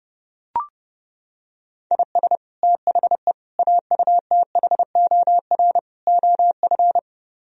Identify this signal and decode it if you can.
Morse code, heard as a keyed tone, about 32 wpm, 700 Hz. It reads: IS THE AUTHOR OF